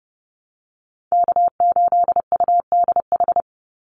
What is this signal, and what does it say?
Morse code, heard as a keyed tone, about 30 wpm, 700 Hz. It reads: K8UDH
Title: Call Signs - 30wpm